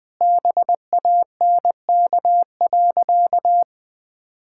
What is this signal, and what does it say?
Morse code, heard as a keyed tone, about 20 wpm, 700 Hz. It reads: BANK.
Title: Sentences from Top 1600 Words 20wpm